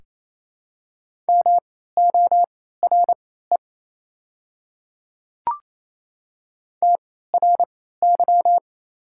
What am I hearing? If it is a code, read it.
Morse code, heard as a keyed tone, about 28 wpm, spaced as for 16 wpm, 700 Hz. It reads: MORE TRY
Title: 3x Character Spacing - Top 200 Words - 28wpm